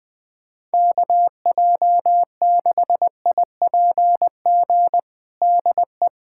Morse code, heard as a keyed tone, about 20 wpm, 700 Hz. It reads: KJ6IPG DE
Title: USA Callsign DE Callsign 20wpm